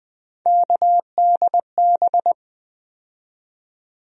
Morse code, heard as a keyed tone, about 20 wpm, 700 Hz. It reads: KDB
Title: USA Call Sign Postfixes - 3 Letters 20wpm